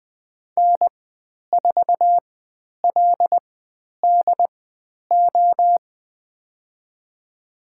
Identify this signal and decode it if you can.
Morse code, heard as a keyed tone, about 20 wpm, spaced as for 10 wpm, 700 Hz. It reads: N4LDO
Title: 095 Lesson 16g - Letter c - Call Signs 20wpm